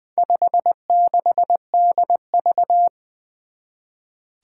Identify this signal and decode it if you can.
Morse code, heard as a keyed tone, about 20 wpm, 700 Hz. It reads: 56DV
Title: Four Challenging Letter-Number Combinations 20wpm